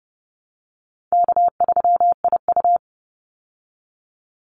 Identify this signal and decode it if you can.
Morse code, heard as a keyed tone, about 30 wpm, 700 Hz. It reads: K3IU
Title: Easy Call Signs Repeated 3x - 30wpm